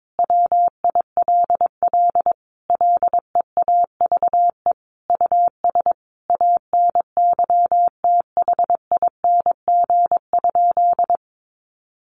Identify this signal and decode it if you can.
Morse code, heard as a keyed tone, about 22 wpm, 700 Hz. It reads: WILL LEAVE US ANYTHING?
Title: Sentences from Top 1100 Words 22wpm